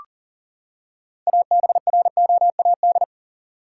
Morse code, heard as a keyed tone, about 40 wpm, 700 Hz. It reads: ABROAD
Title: Six Letter Words - 40wpm